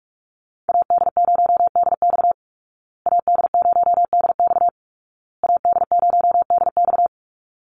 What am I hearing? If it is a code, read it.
Morse code, heard as a keyed tone, about 44 wpm, 700 Hz. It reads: AD0DX AD0DX AD0DX
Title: Call Signs Repeated 2x - 45wpm